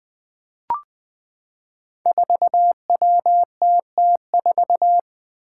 Morse code, heard as a keyed tone, about 20 wpm, 700 Hz. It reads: 4WTT4
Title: Five Challenging Letter-Number Combinations 20wpm